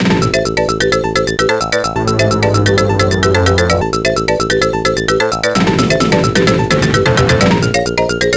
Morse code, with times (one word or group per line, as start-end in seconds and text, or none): none